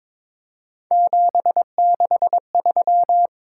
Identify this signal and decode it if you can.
Morse code, heard as a keyed tone, about 23 wpm, 700 Hz. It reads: 763
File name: practice / 22wpm / Three Letter-Number Combinations - 22wpm